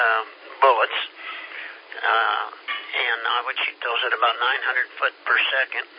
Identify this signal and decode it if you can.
Emergency Alert System (SAME) tones, heard as harsh buzzing bursts, with speech. It is heard from 1.1 to 4.9 s.